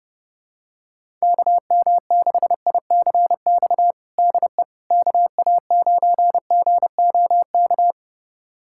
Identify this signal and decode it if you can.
Morse code, heard as a keyed tone, about 32 wpm, 700 Hz. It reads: KM6ICX DE KA9GOK
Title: USA Callsign DE Callsign - 6 Characters - 30wpm